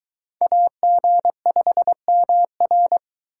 Morse code, heard as a keyed tone, about 22 wpm, 700 Hz. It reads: AG5MR